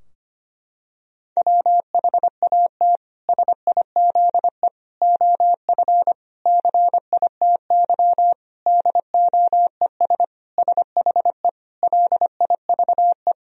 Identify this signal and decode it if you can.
Morse code, heard as a keyed tone, about 25 wpm, 700 Hz. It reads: WHAT SIZE OF CITY DOES SHE LIVE
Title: Sentences from Top 1000 Words - Review Part 2 - 25wpm